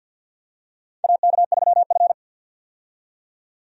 Morse code, heard as a keyed tone, about 52 wpm, 700 Hz. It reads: AK3F
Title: USA Call Signs - 4 Characters - 50wpm